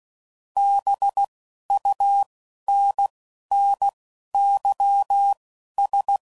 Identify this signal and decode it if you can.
Morse code, heard as a keyed tone, about 16 wpm, spaced as for 11 wpm, 800 Hz. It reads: BUNNYS